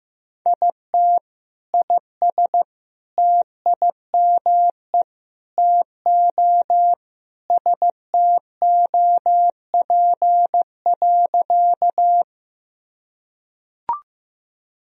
Morse code, heard as a keyed tone, about 15 wpm, 700 Hz. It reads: IT IS TIME TO STOP.
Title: Sentences from Top 1000 Words - Review Part 1 - 15wpm